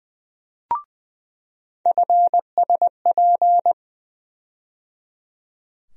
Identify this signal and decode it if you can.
Morse code, heard as a keyed tone, about 21 wpm, 700 Hz. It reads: FSP
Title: USA Call Sign Postfixes - 3 Letters 20wpm